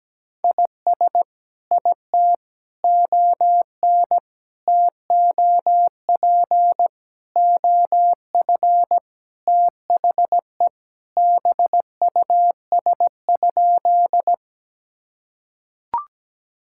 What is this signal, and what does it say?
Morse code, heard as a keyed tone, about 17 wpm, 700 Hz. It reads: IS IT ON TOP OF THE BUS?
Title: Sentences from Top 1000 Words - Review Part 6 - 17wpm